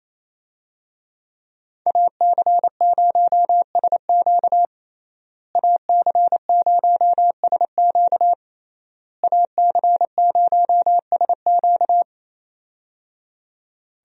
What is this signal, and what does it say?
Morse code, heard as a keyed tone, about 28 wpm, 700 Hz. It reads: AC0SQ AC0SQ AC0SQ
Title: USA Call Signs - 5 Characters - Repeated 3x - 28wpm